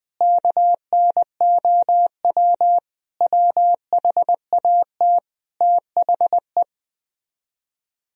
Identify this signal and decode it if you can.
Morse code, heard as a keyed tone, about 20 wpm, 700 Hz. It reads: KNOW WHAT THE
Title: Speed-Racing - Sets of 5 Words 17wpm